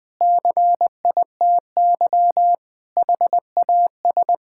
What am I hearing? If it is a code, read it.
Morse code, heard as a keyed tone, about 20 wpm, 700 Hz. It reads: CITY HAS
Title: Sentences with 5-Word Phrases - 20wpm